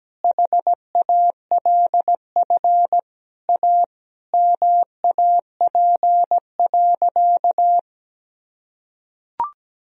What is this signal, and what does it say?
Morse code, heard as a keyed tone, about 17 wpm, 700 Hz. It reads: HALF A MAP.
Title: Sentences from Top 1000 Words - Review Part 2 - 17wpm